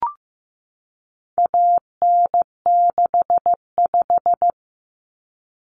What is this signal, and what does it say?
Morse code, heard as a keyed tone, about 15 wpm, 700 Hz. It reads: AN65
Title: Common Cut Numbers - 4-Digit 15wpm